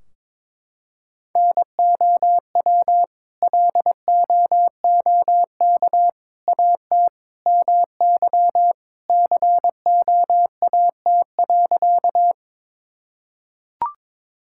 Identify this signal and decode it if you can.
Morse code, heard as a keyed tone, about 21 wpm, 700 Hz. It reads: NOW LOOK AT MY COAT.
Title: Sentences from Top 1600 Words 22wpm